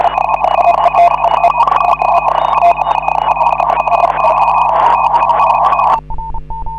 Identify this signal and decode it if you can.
Morse code, heard as a keyed tone, about 32 wpm, 2500 Hz, with noise. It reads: HA5MY HA5MY TEST